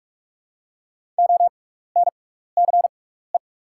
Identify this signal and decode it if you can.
Morse code, heard as a keyed tone, about 44 wpm, spaced as for 15 wpm, 690 Hz. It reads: ONCE
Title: 6x Character Spacing - Top 300 Words - 45wpm